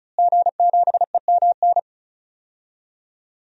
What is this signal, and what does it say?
Morse code, heard as a keyed tone, about 34 wpm, 700 Hz. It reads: G7EMN
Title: DXCC-100 Call Signs - 5 Characters - Repeated 3x - 35wpm